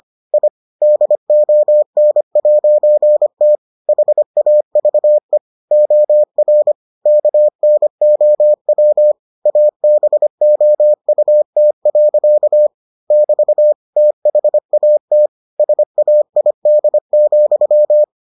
Morse code, heard as a keyed tone, about 25 wpm, 600 Hz. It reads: I DON'T HAVE OR KNOW ABOUT. <BT> THAT SAID,